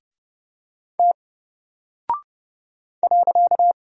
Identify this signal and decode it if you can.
Morse code, heard as a keyed tone, about 32 wpm, 700 Hz. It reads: T .
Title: Single Character (Letter:Number:Punctuation:Prosign) 30wpm